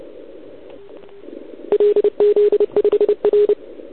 Morse code, heard as a keyed tone, about 28 wpm, 410 Hz. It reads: LZ5R
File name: ARCHIV / OKDX2019 / OKOM40 KX3